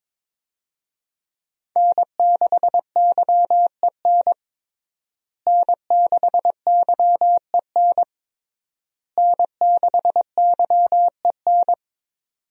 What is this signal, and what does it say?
Morse code, heard as a keyed tone, about 22 wpm, 700 Hz. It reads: N6YEN N6YEN N6YEN
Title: USA Call Signs - 5 Characters - Repeated 3x - 22wpm